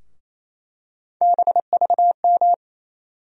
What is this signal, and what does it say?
Morse code, heard as a keyed tone, about 28 wpm, 700 Hz. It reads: BVM
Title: USA Call Sign Postfixes - 3 Letters 28wpm